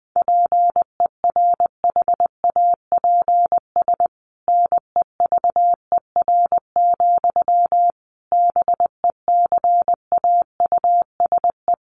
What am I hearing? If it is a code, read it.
Morse code, heard as a keyed tone, about 20 wpm, 700 Hz. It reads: PERHAPS NEVER, BECAUSE
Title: Sentences from Top 1200 Words 20wpm